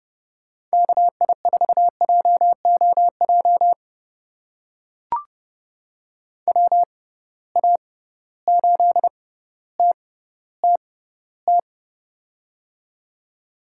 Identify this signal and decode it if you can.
Morse code, heard as a keyed tone, about 31 wpm, spaced as for 15 wpm, 700 Hz. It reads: KI4JOJ WA8TTT
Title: USA Call Signs - 6 Characters - 6x Character Spacing - 30wpm